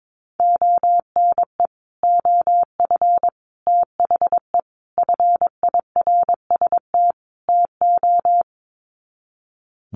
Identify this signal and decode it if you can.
Morse code, heard as a keyed tone, about 22 wpm, 700 Hz. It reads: ONE OF THE FIRST TO
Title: Sets of 5 Words Encore 22wpm